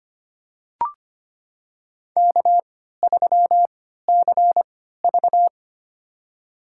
Morse code, heard as a keyed tone, about 25 wpm, spaced as for 14 wpm, 700 Hz. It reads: K3CV
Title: USA Call Signs - 4 Characters - 3x Character Spacing - 25wpm